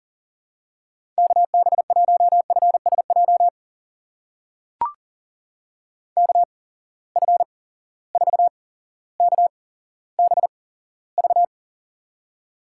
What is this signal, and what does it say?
Morse code, heard as a keyed tone, about 37 wpm, spaced as for 17 wpm, 700 Hz. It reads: KB1FSJ KF4KBV
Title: USA Call Signs - 6 Characters - 8x Character Spacing - 40wpm